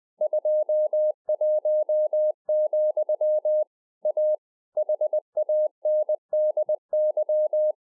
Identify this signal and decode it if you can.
Morse code, heard as a keyed tone, about 20 wpm, 610 Hz. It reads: 21, A HANDY
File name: 2011-04-mp3 / A SMALL TRANCEIVER 3 20WPM